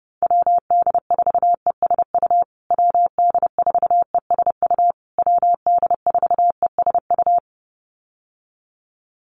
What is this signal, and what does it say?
Morse code, heard as a keyed tone, about 30 wpm, 700 Hz. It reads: WD4ESU WD4ESU WD4ESU
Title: USA Call Signs Repeated 3x 30wpm